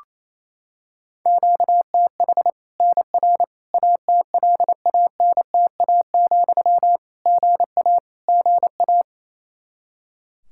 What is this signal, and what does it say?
Morse code, heard as a keyed tone, about 28 wpm, 700 Hz. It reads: QTH NR ATLANTA, GA GA
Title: K1USN SST QSO Practice 28wpm